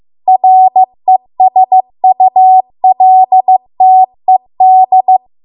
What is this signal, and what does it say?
Morse code, heard as a keyed tone, about 15 wpm, 750 Hz. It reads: RESULTED